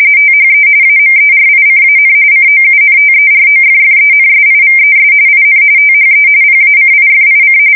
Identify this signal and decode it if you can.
RTTY, about 45.45 baud. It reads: THE AMERICAN COUNTEROFFENSIVES IN THE PACIFIC
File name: fsk170 / battleshipnj